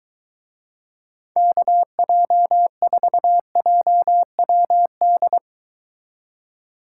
Morse code, heard as a keyed tone, about 22 wpm, 700 Hz. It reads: KJ4JWD